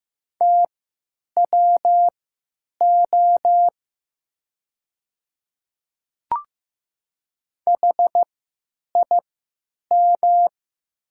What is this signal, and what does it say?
Morse code, heard as a keyed tone, about 15 wpm, spaced as for 9 wpm, 700 Hz. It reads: TWO HIM
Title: 3x Character Spacing - Top 100 Words - 15wpm